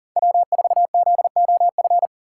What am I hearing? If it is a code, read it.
Morse code, heard as a keyed tone, about 39 wpm, 700 Hz. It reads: W4ZOF
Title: Code Groups - 5 Letter:Numbers - 40wpm